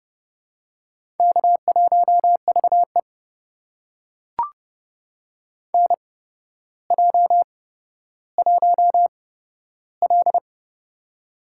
Morse code, heard as a keyed tone, about 30 wpm, spaced as for 13 wpm, 700 Hz. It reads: K1VE NJ1L
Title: USA Call Signs - 4 Characters - 8x Character Spacing - 30wpm